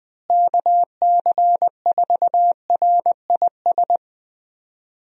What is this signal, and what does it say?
Morse code, heard as a keyed tone, about 21 wpm, 700 Hz. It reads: KC4RIS
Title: USA Call Signs - 6 Characters - Repeated 3x - 20wpm